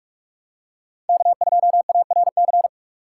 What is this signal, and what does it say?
Morse code, heard as a keyed tone, about 43 wpm, 700 Hz. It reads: K2ARC